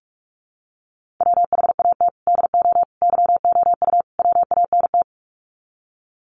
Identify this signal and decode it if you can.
Morse code, heard as a keyed tone, about 44 wpm, 700 Hz. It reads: WHAT DO YOU WANT
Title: Sets of 4 Words 45wpm